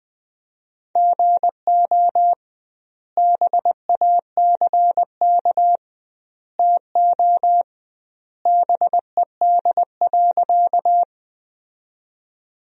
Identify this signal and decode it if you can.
Morse code, heard as a keyed tone, about 20 wpm, 700 Hz. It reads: GO BACK TO BED.
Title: 2x Word-Spacing - Sentences from Top 500 Words - 20wpm